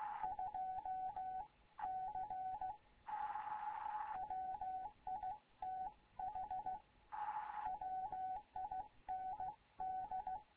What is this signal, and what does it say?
Morse code, heard as a keyed tone, about 16 wpm, 750 Hz. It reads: 2 C WITH WIND
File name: Morze-HA2MN / hangp / 2 11 GYAKORLAT